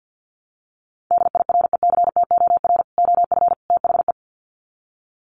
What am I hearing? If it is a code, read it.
Morse code, heard as a keyed tone, about 50 wpm, 700 Hz. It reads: DIRECTOR OF THE
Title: Sets of 3 Words 50wpm